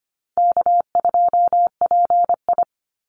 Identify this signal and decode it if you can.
Morse code, heard as a keyed tone, about 25 wpm, 700 Hz. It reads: K2PI
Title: Easy Call Signs Repeated 2x - 25wpm